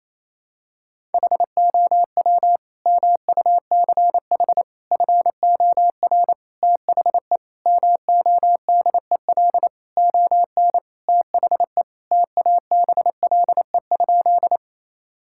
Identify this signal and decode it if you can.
Morse code, heard as a keyed tone, about 28 wpm, 700 Hz. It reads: HOW MUCH FOR THE MODEL ON THE TABLE?
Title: Sentences from Top 1000 Words - Review Part 5 - 28wpm